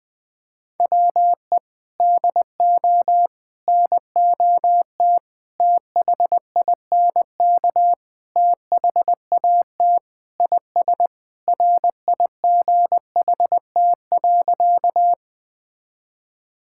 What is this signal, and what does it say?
Morse code, heard as a keyed tone, about 21 wpm, 700 Hz. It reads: WE DO NOT THINK THAT IS RIGHT.
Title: Sentences with 4-Word Phrases - 20wpm